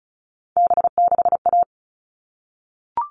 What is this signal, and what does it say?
Morse code, heard as a keyed tone, about 34 wpm, 700 Hz. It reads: B6A
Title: All Cut Numbers - 3-Digit 35wpm